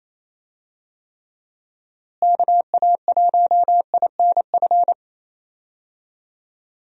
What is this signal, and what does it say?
Morse code, heard as a keyed tone, about 28 wpm, 700 Hz. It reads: KA1INF
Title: USA Call Signs - 6 Characters - Repeated 3x - 28wpm